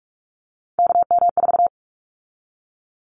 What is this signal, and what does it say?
Morse code, heard as a keyed tone, about 46 wpm, 700 Hz. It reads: KM4
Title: Callsign Prefixes for Top 25 DXCC - 45wpm